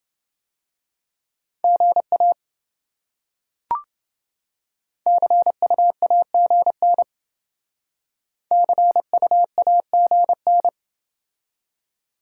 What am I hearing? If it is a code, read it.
Morse code, heard as a keyed tone, about 28 wpm, 700 Hz. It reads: GA CUAGN CUAGN